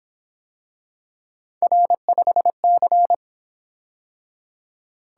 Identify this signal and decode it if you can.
Morse code, heard as a keyed tone, about 26 wpm, 700 Hz. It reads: R5C
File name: Speed-Racing - Three Letter-Number Combinations - 22wpm